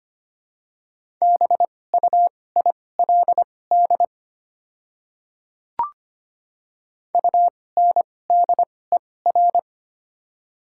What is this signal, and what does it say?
Morse code, heard as a keyed tone, about 25 wpm, spaced as for 18 wpm, 690 Hz. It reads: BUILD UNDER
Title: 2x Character Spacing - Top 200 Words - 25wpm